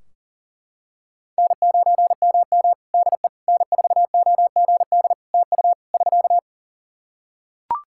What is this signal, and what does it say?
Morse code, heard as a keyed tone, about 40 wpm, 700 Hz. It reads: N9MM DE N4OGD TU <SK>
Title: K1USN SST QSO Practice 40wpm